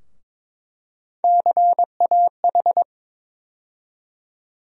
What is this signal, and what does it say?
Morse code, heard as a keyed tone, about 22 wpm, 700 Hz. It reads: CAH
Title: USA Call Sign Postfixes - 3 Letters 22wpm